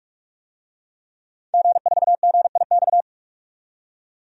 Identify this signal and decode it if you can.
Morse code, heard as a keyed tone, about 42 wpm, 700 Hz. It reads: G4GIX
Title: DXCC-100 Call Signs - 5 Characters - 45wpm